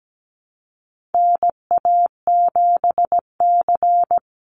Morse code, heard as a keyed tone, about 17 wpm, 700 Hz. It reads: NA7C
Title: Call Signs - 17wpm